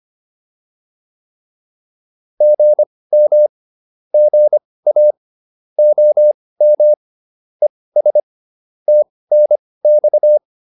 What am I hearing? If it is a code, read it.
Morse code, heard as a keyed tone, about 25 wpm, spaced as for 18 wpm, 600 Hz. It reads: GM GA OM ES TNX
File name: QSO410 18